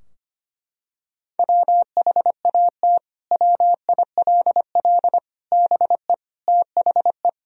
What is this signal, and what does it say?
Morse code, heard as a keyed tone, about 25 wpm, 700 Hz. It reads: WHAT WILL BE THE
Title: Sentences from Top 1000 Words - Review Part 2 - 25wpm